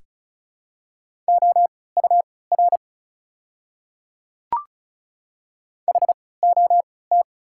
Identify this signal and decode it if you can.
Morse code, heard as a keyed tone, about 36 wpm, spaced as for 20 wpm, 700 Hz. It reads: OUR HOT
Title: 3x Character Spacing - Top 300 Words - 35wpm